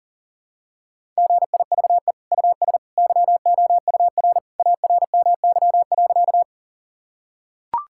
Morse code, heard as a keyed tone, about 40 wpm, 700 Hz. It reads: GIVE US YOUR ARMY.
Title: Sentences from Top 1000 Words - Review Part 5 - 40wpm